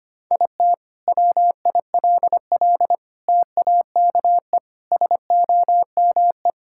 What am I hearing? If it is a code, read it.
Morse code, heard as a keyed tone, about 25 wpm, 700 Hz. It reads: IT WILL TAKE SOME
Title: Sentences from Top 1000 Words - Review Part 5 - 25wpm